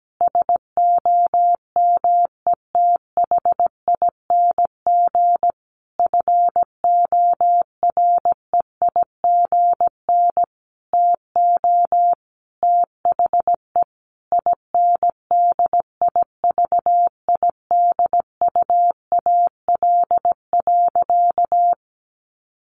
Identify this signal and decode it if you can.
Morse code, heard as a keyed tone, about 17 wpm, 700 Hz. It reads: SOMETHING FOREIGN TO THE INDIVIDUAL.
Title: Sentences from Top 1400 Words 17wpm